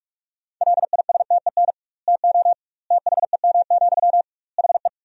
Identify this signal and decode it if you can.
Morse code, heard as a keyed tone, about 46 wpm, 700 Hz. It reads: LISTEN TO THEM, HE
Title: 2x Word-Spacing - Sentences from Top 400 Words - 45wpm